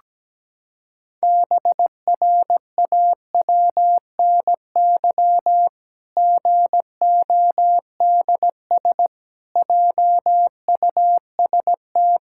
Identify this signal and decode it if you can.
Morse code, heard as a keyed tone, about 17 wpm, 700 Hz. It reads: BRAWNY GODS JUST
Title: Pangram Challenge v2 - 17wpm